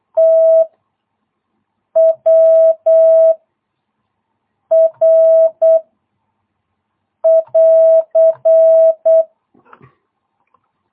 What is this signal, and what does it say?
Morse code, heard as a keyed tone, about 8 wpm, 660 Hz. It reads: T W R <AR>